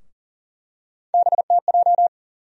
Morse code, heard as a keyed tone, about 40 wpm, 700 Hz. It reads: BTJ
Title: USA Call Sign Postfixes - 3 Letters 40wpm